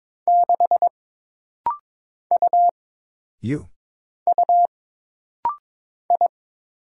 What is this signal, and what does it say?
Morse code, heard as a keyed tone, about 23 wpm, 700 Hz. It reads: 6 U U I
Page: This rapid-fire timing is geared to push your ICR (Instant Character Recognition) skills.